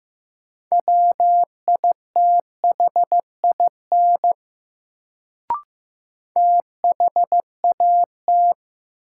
Morse code, heard as a keyed tone, about 15 wpm, 700 Hz. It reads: WITHIN THAT
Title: Sets of 2 Words - New Word Drill - 15wpm